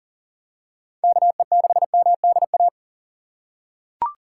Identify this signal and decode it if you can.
Morse code, heard as a keyed tone, about 37 wpm, 700 Hz. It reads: KE6MDA
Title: USA Call Signs - 6 Characters - 3x Character Spacing - 40wpm